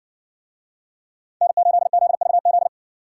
Morse code, heard as a keyed tone, about 60 wpm, 700 Hz. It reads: N8ZVZ